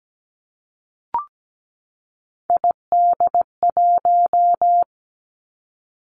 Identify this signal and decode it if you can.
Morse code, heard as a keyed tone, about 17 wpm, 700 Hz. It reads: ID1
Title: Three Challenging Letter-Number Combinations 17wpm